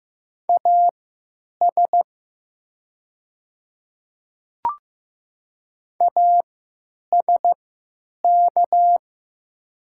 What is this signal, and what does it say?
Morse code, heard as a keyed tone, about 15 wpm, spaced as for 9 wpm, 690 Hz. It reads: AS ASK
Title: 3x Character Spacing - Top 300 Words - 15wpm